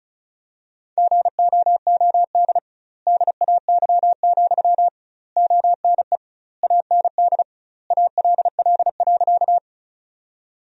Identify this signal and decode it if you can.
Morse code, heard as a keyed tone, about 34 wpm, 700 Hz. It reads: GOOD DAY, ONE AND ALL.
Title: 2x Word-Spacing - Sentences from Top 200 Words - 35wpm